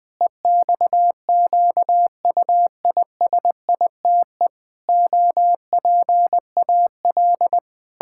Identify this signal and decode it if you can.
Morse code, heard as a keyed tone, about 20 wpm, 700 Hz. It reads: EXQUISITE OPAL
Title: Pangram Challenge v2 - 20wpm